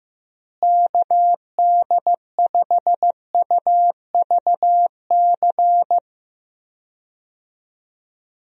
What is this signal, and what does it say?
Morse code, heard as a keyed tone, about 16 wpm, 700 Hz. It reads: KD5UVC
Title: USA Call Signs - 6 Characters - Repeated 3x - 15wpm